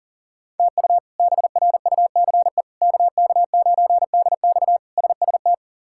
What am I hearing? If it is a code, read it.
Morse code, heard as a keyed tone, about 37 wpm, 700 Hz. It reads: TU BRUCE KK9DX SST
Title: K1USN SST QSO Practice - 40wpm